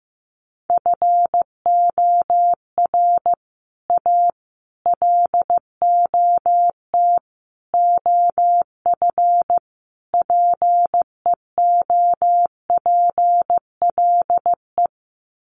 Morse code, heard as a keyed tone, about 15 wpm, 700 Hz. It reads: FOR A LOT OF PEOPLE
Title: Sets of 5 Words Encore 15wpm